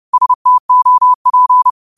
Morse code, heard as a keyed tone, about 30 wpm, 1000 Hz. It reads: STOP
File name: qotdfiles / 30wpm